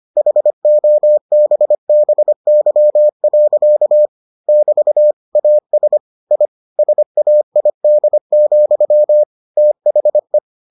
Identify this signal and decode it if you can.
Morse code, heard as a keyed tone, about 25 wpm, 600 Hz. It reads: HOBBY. <BT> AS I SAID, THE